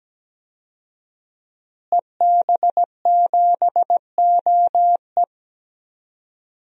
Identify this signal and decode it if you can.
Morse code, heard as a keyed tone, about 17 wpm, 700 Hz. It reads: EB7OE
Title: DXCC-100 Call Signs - 5 Characters - Repeated 2x - 17wpm